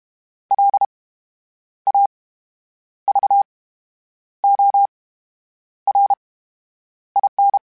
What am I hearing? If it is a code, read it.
Morse code, heard as a keyed tone, about 32 wpm, spaced as for 9 wpm, 800 Hz. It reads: LAVORIN
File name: Special INGs / Special INGs-mp3s / VerySlowToFaster Ing @30